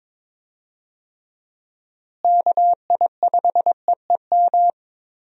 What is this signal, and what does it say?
Morse code, heard as a keyed tone, about 22 wpm, 700 Hz. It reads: KI5EEM
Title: USA Call Signs - 6 Characters - Repeated 2x - 22wpm